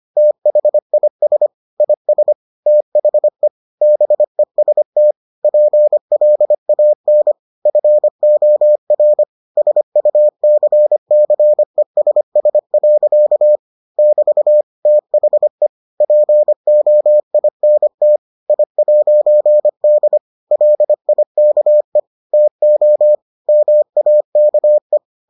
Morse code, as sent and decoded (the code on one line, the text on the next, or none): - .... .. ... / .. ... / - .... / -... ... - / .--. .-.. .- -. / ..-. --- .-. / ... ..- -.-. -.-. ... ... .-.-.- / -...- / - .... / .--. --- .. -. - / .. .----. -.. / .-.. .. -.- . / - --- / -- .- -.- .
THIS IS THE BEST PLAN FOR SUCCESS. <BT> THE POINT I'D LIKE TO MAKE